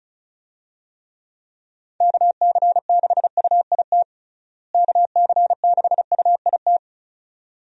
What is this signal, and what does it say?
Morse code, heard as a keyed tone, about 34 wpm, 700 Hz. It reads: KC6UIT KC6UIT
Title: USA Call Signs - 6 Characters - Repeated 2x - 35wpm